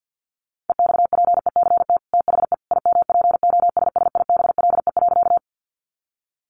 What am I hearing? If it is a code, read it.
Morse code, heard as a keyed tone, about 50 wpm, 700 Hz. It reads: EXPECT THE IMPOSSIBLE.
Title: Sentences from Top 1100 Words 50wpm